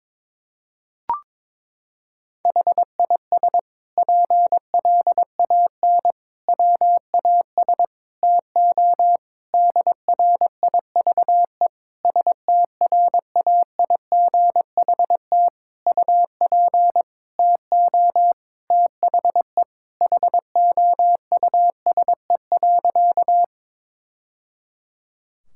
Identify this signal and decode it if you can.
Morse code, heard as a keyed tone, about 22 wpm, 700 Hz. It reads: HIS PLAN WAS TO DRIVE STRAIGHT UP TO THE HOUSE.
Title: Sentences from Top 1000 Words - Review Part 6 - 22wpm